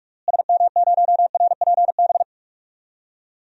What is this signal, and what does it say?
Morse code, heard as a keyed tone, about 46 wpm, 700 Hz. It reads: SM0RPB
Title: DXCC-100 Call Signs - 6 Characters - Repeated 3x - 45wpm